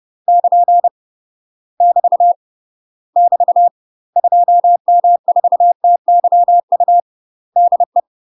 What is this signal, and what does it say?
Morse code, heard as a keyed tone, about 30 wpm, 700 Hz. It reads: <KN> <BT> <BT> 2M4TYU DE
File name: UK MASTER QSOs TRAINING / 30WPM / UK MASTER QSOs 0029